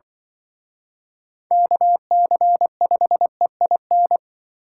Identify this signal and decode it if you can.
Morse code, heard as a keyed tone, about 25 wpm, 700 Hz. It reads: KC5EIN
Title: Speed Racing - USA 2x3 Call Signs 20wpm